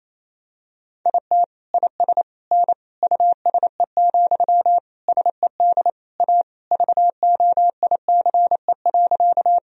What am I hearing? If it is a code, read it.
Morse code, heard as a keyed tone, about 28 wpm, 700 Hz. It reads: IT IS N USE, SED A VOICE.
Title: Sentences from Top 500 Words and Abbreviations 28wpm